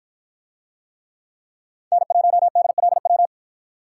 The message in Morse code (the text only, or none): N1DLW